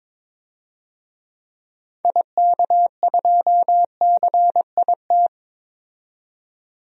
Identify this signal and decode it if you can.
Morse code, heard as a keyed tone, about 22 wpm, 700 Hz. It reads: IK2CIT
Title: DXCC-100 Call Signs - 6 Characters - 22wpm